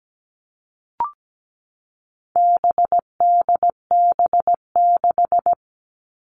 Morse code, heard as a keyed tone, about 17 wpm, 700 Hz. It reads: BDB6
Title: All Cut Numbers - 4-Digit 17wpm